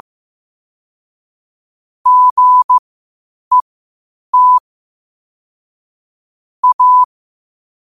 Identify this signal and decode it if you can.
Morse code, heard as a keyed tone, about 15 wpm, spaced as for 8 wpm, 1000 Hz. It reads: GET A
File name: 8wpm